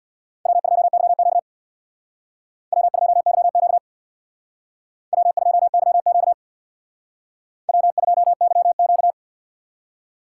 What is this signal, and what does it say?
Morse code, heard as a keyed tone, about 64 wpm, 700 Hz. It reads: W2YQ W2YQ W2YQ W2YQ